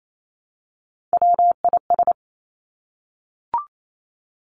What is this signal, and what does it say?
Morse code, heard as a keyed tone, about 28 wpm, 700 Hz. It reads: WIS
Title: Three Challenging Letter-Number Combinations 28wpm